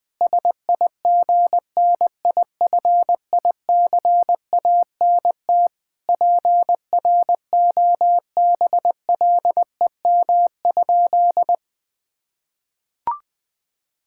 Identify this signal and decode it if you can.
Morse code, heard as a keyed tone, about 20 wpm, 700 Hz. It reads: SIGNIFICANT PROBLEM?
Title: Sentences from Top 1000 Words - Review Part 6 - 20wpm